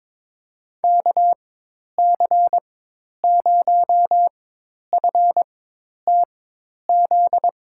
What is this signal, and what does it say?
Morse code, heard as a keyed tone, about 22 wpm, spaced as for 10 wpm, 700 Hz. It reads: KC0FTZ
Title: USA Call Signs - 6 Characters - 4x Character Spacing - 22wpm